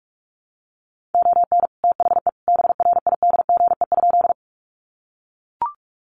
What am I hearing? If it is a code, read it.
Morse code, heard as a keyed tone, about 44 wpm, 700 Hz. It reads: ON THE BRIDGE?
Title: Sentences from Top 1200 Words 45wpm